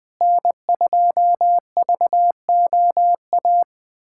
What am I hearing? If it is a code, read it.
Morse code, heard as a keyed tone, about 20 wpm, 700 Hz. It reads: N2VOA
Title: USA Callsign DE Callsign - 5 Characters - 20wpm